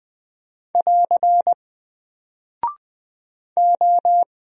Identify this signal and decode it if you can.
Morse code, heard as a keyed tone, about 20 wpm, 700 Hz. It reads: <AR> O
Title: Single Character (Letter:Number:Punctuation:Prosign) Alt 20wpm